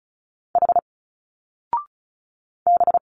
Morse code, heard as a keyed tone, about 34 wpm, 700 Hz. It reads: H B
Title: Single Letters 35wpm